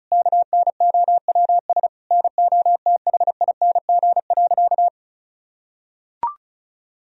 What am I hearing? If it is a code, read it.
Morse code, heard as a keyed tone, about 36 wpm, 700 Hz. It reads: KNOWS NOTHING.
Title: Sentences from Top 1000 Words - Review Part 1 - 35wpm